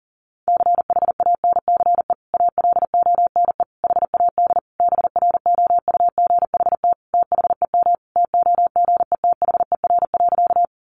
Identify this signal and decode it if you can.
Morse code, heard as a keyed tone, about 40 wpm, 700 Hz. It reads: CHANCE ALONE HAD BROUGHT THEM TOGETHER.
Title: Sentences from Top 1000 Words 40wpm